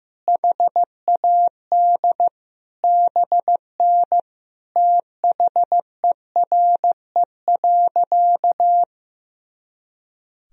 Morse code, heard as a keyed tone, about 15 wpm, 700 Hz. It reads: HAD BN THERE.
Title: Sentences from Top 300 Words and Abbreviations 15wpm